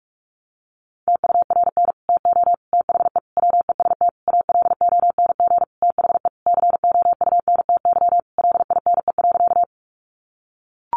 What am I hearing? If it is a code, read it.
Morse code, heard as a keyed tone, about 44 wpm, 700 Hz. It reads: TURN TO THE WEST ALONG THE COUNTY LINE.
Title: Sentences from Top 1200 Words 45wpm